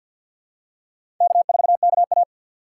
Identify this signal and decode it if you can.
Morse code, heard as a keyed tone, about 52 wpm, 700 Hz. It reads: K4KA
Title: USA Call Signs - 6x Character Spacing - 50wpm